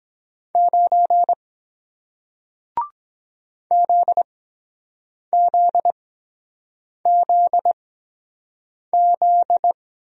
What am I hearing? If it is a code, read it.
Morse code, heard as a keyed tone, about 23 wpm, spaced as for 7 wpm, 700 Hz. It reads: 9 ZZZZ